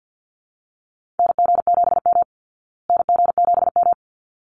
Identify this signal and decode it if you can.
Morse code, heard as a keyed tone, about 50 wpm, 700 Hz. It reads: NG7M NG7M
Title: Call Signs Repeated 3x - 50wpm